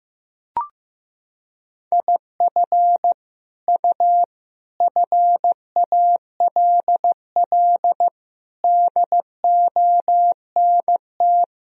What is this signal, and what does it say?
Morse code, heard as a keyed tone, about 16 wpm, 700 Hz. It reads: IF U FALL DONT
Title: Sentences from Top 500 Words and Abbreviations 15wpm